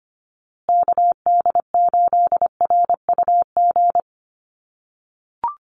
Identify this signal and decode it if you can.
Morse code, heard as a keyed tone, about 25 wpm, 700 Hz. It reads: KD8RUG
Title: USA Call Signs Repeated 2x 25wpm